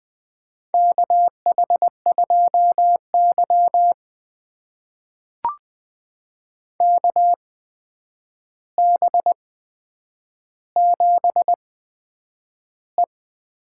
Morse code, heard as a keyed tone, about 20 wpm, spaced as for 9 wpm, 700 Hz. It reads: KH2Y KB7E
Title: USA Call Signs - 4 Characters - 8x Character Spacing - 20wpm